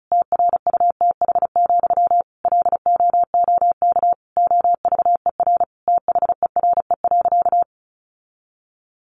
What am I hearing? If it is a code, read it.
Morse code, heard as a keyed tone, about 34 wpm, 700 Hz. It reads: TRUTH, LOOK OVER THERE.
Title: Sentences from Top 800 Words 35wpm